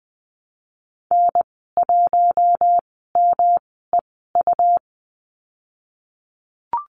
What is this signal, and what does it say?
Morse code, heard as a keyed tone, about 20 wpm, spaced as for 15 wpm, 700 Hz. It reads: N1MEU
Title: 122 Lesson 20g - Number 3 - Call Signs 20wpm